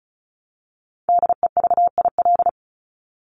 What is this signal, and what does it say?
Morse code, heard as a keyed tone, about 34 wpm, 700 Hz. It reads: DEVIL
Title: Top 1751-2000 Words 35wpm